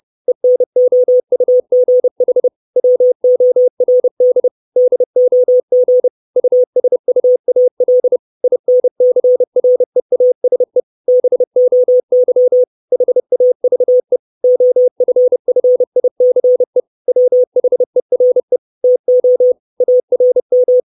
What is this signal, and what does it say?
Morse code, heard as a keyed tone, about 30 wpm, 500 Hz. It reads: ENOUGH WORD DOG USUAL INCREASE BOY HAVE OFFICE WHERE TO ARM
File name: 632 Words 30WPM